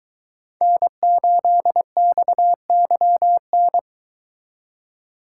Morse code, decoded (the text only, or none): N8XYN